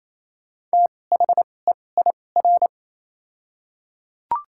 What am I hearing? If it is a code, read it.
Morse code, heard as a keyed tone, about 28 wpm, spaced as for 20 wpm, 700 Hz. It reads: THEIR
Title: 2x Character Spacing - Top 100 Words - 28wpm